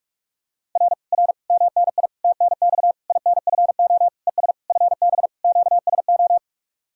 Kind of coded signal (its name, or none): Morse code